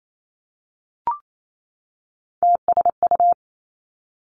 Morse code, heard as a keyed tone, about 28 wpm, 700 Hz. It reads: TSU
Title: Three Challenging Letter-Number Combinations 28wpm